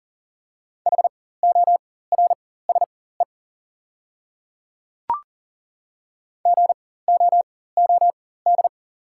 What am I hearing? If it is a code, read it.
Morse code, heard as a keyed tone, about 38 wpm, spaced as for 18 wpm, 700 Hz. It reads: HORSE GOOD
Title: 4x Character Spacing - Top 300 Words - 40wpm